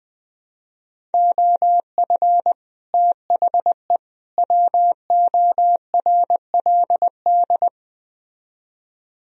Morse code, heard as a keyed tone, about 21 wpm, 700 Hz. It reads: OF THE WORLD